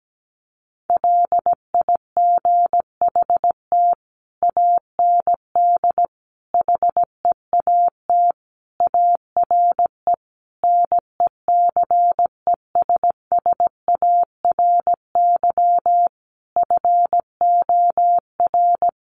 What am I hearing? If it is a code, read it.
Morse code, heard as a keyed tone, about 17 wpm, 700 Hz. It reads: LIGHT AND HEAT ARE NECESSARY FOR
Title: Sentences from Top 1300 Words 17wpm